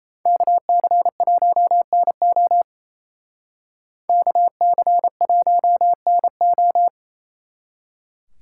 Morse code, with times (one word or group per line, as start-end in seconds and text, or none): 0.3-2.6 KC1NO
4.1-6.9 KC1NO